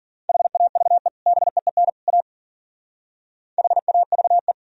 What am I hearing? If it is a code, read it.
Morse code, heard as a keyed tone, about 48 wpm, 700 Hz. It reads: HAVE BEEN A HAVE
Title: Speed-Racing - Sets of 3 Words 40wpm